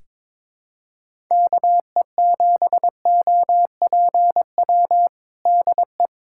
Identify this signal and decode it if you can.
Morse code, heard as a keyed tone, about 22 wpm, 700 Hz. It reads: KE7OPW DE
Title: USA Callsign DE Callsign - 6 Characters - 22wpm